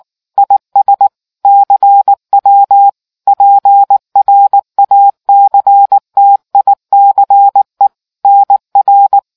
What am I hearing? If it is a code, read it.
Morse code, heard as a keyed tone, about 19 wpm, 800 Hz. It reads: IS CW PRACTICE NR